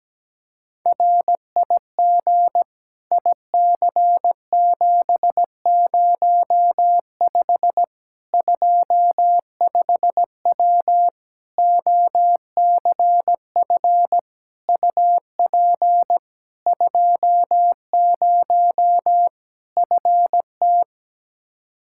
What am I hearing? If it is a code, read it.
Morse code, heard as a keyed tone, about 17 wpm, 700 Hz. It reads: RIG IC705 25W OCF UP 20 FT
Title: K1USN SST QSO Practice 17wpm